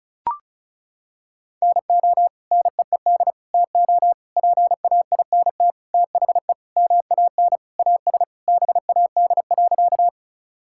Morse code, heard as a keyed tone, about 34 wpm, 700 Hz. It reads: NO NEED TO PAINT THE MAN AS BAD.
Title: Sentences from Top 1000 Words - Review Part 6 - 35wpm